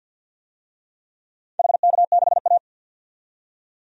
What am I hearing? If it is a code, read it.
Morse code, heard as a keyed tone, about 51 wpm, 700 Hz. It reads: HK6A
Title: DXCC-100 Call Signs - 4 Characters - 50wpm